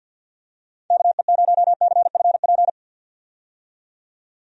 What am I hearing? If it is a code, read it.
Morse code, heard as a keyed tone, about 45 wpm, 700 Hz. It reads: KE0CFP
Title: USA Call Signs - 6 Characters - 50wpm